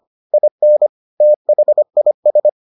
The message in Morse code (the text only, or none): IN THIS